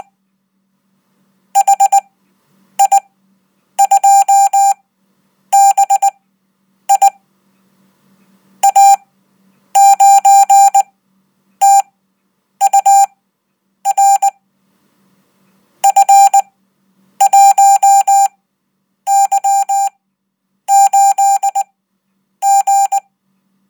Morse code, heard as a keyed tone, about 20 wpm, spaced as for 8 wpm, 790 Hz. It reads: HI2BI A9TUR F1Y8G
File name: Code